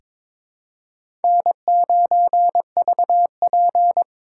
Morse code, heard as a keyed tone, about 22 wpm, 700 Hz. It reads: N9VP
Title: USA Call Signs - 3x Character Spacing - 22wpm